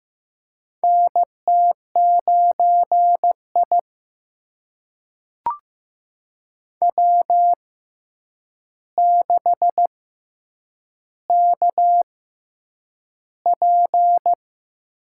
Morse code, heard as a keyed tone, about 15 wpm, spaced as for 8 wpm, 700 Hz. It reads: NT9I W6KP
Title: USA Call Signs - 6x Character Spacing - 15wpm